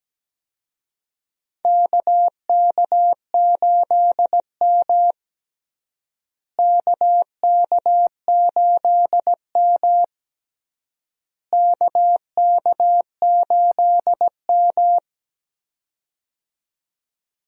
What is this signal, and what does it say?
Morse code, heard as a keyed tone, about 17 wpm, 700 Hz. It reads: KK8M KK8M KK8M
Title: USA Call Signs - 4 Characters - Repeated 3x - 17wpm